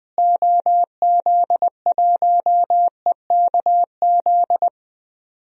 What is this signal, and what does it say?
Morse code, heard as a keyed tone, about 20 wpm, 700 Hz. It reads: OZ1EKZ
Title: DXCC-100 Call Signs - 6 Characters - Repeated 3x - 20wpm